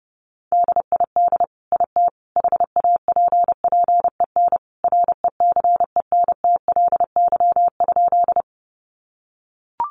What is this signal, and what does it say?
Morse code, heard as a keyed tone, about 30 wpm, 700 Hz. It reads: DID IT HAPPEN RECENTLY?
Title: Sentences from Top 1200 Words 30wpm